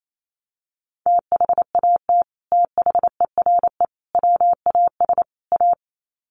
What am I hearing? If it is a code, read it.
Morse code, heard as a keyed tone, about 28 wpm, 700 Hz. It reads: THAT THERE WAS A
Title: Sets of 4 Words 28wpm